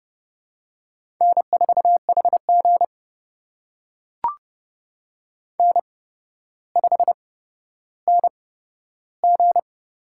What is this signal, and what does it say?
Morse code, heard as a keyed tone, about 30 wpm, spaced as for 13 wpm, 700 Hz. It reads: N4HG N5NG
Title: USA Call Signs - 4 Characters - 8x Character Spacing - 30wpm